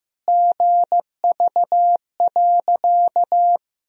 Morse code, heard as a keyed tone, about 15 wpm, 700 Hz. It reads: GV.
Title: Sentences from Top 200 Words and Abbreviations 15wpm